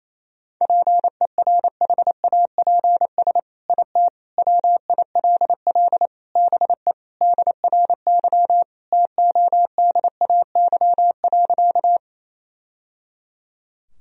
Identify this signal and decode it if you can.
Morse code, heard as a keyed tone, about 28 wpm, 700 Hz. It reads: PERHAPS IT WILL BE DRY TODAY.
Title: Sentences from Top 1000 Words - Review Part 3 - 28wpm